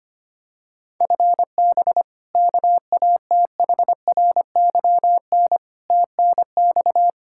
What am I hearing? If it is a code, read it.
Morse code, heard as a keyed tone, about 25 wpm, 700 Hz. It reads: FB KATHRYN TNX
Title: Practice QSO - Exchange 3 - More Info - 25wpm